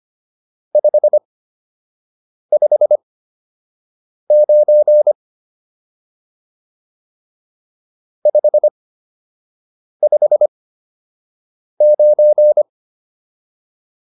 Morse code, heard as a keyed tone, about 25 wpm, spaced as for 6 wpm, 600 Hz. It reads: 559 559